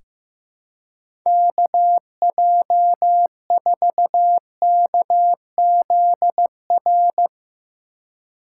Morse code, heard as a keyed tone, about 15 wpm, 700 Hz. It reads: KJ4KZR
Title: USA Call Signs - 6 Characters - 2x Character Spacing - 15wpm